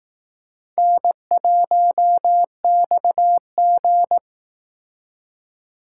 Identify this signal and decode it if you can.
Morse code, heard as a keyed tone, about 19 wpm, 700 Hz. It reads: N1XG
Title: Speed Racing - USA 1x2 Call Signs 15wpm